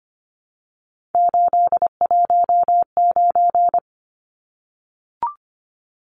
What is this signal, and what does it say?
Morse code, heard as a keyed tone, about 25 wpm, 690 Hz. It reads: 819
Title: Numbers - 3-Digit 25wpm